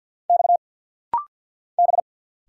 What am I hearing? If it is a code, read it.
Morse code, heard as a keyed tone, about 50 wpm, 690 Hz. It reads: X B
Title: Single Letter-Number Alt - Rapid-Fire 50wpm